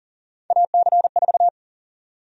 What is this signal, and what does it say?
Morse code, heard as a keyed tone, about 40 wpm, 700 Hz. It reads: AC4
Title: USA Call Sign Prefixes - 40wpm